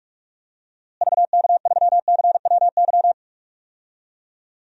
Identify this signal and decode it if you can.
Morse code, heard as a keyed tone, about 46 wpm, 700 Hz. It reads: VK3CWY
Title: DXCC-100 Call Signs - 6 Characters - Repeated 2x - 45wpm